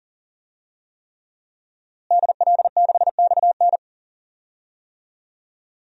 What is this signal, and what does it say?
Morse code, heard as a keyed tone, about 40 wpm, 700 Hz. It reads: DL6XN
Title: DXCC-100 Call Signs - 5 Characters - Repeated 3x - 40wpm